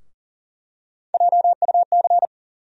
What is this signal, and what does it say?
Morse code, heard as a keyed tone, about 40 wpm, 700 Hz. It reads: JUC
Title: USA Call Sign Postfixes - 3 Letters 40wpm